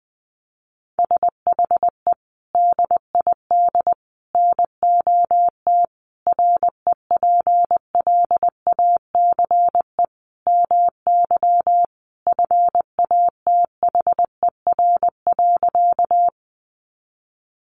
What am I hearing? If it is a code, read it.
Morse code, heard as a keyed tone, about 20 wpm, 700 Hz. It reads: SHE DID NOT REPLACE MY FATHER.
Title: Sentences from Top 1400 Words 20wpm